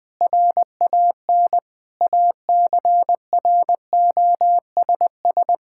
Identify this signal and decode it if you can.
Morse code, heard as a keyed tone, about 21 wpm, 700 Hz. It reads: RAN ACROSS
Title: Sentences from Top 1000 Words - Review Part 5 - 20wpm